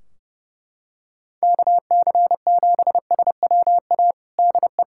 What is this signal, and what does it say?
Morse code, heard as a keyed tone, about 30 wpm, 700 Hz. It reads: KC7SWA DE
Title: Practice QSO - Exchange 4 - Ending - 30wpm